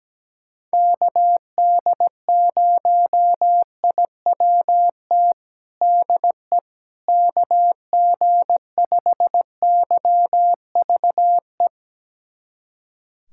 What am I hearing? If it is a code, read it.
Morse code, heard as a keyed tone, about 17 wpm, 700 Hz. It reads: KD0IWT DE KG5YVE
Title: USA Callsign DE Callsign - 6 Characters - 17wpm